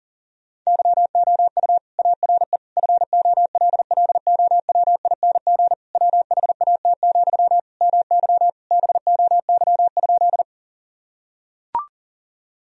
Morse code, heard as a keyed tone, about 40 wpm, 690 Hz. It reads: YOU ARE FOLLOWING WHAT, MY BOY?
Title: Sentences from Top 1000 Words - Review Part 4 - 40wpm